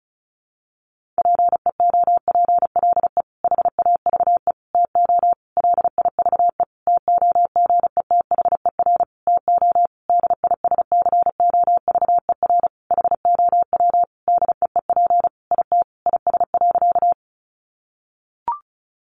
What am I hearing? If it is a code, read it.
Morse code, heard as a keyed tone, about 34 wpm, 700 Hz. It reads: PEOPLE HAVE TO LIVE TOGETHER TO DISCOVER HOW DEEP IT IS.
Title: Sentences from Top 1100 Words 35wpm